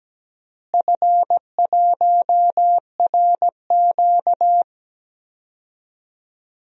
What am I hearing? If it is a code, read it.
Morse code, heard as a keyed tone, about 17 wpm, 700 Hz. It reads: F1RQ
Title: DXCC-100 Call Signs - 4 Characters - Repeated 2x - 17wpm